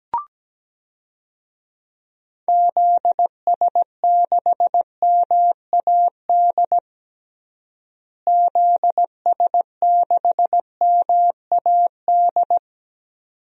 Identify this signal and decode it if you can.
Morse code, heard as a keyed tone, about 17 wpm, 700 Hz. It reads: ZS6MAD ZS6MAD
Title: DXCC-100 Call Signs - 6 Characters - Repeated 2x - 17wpm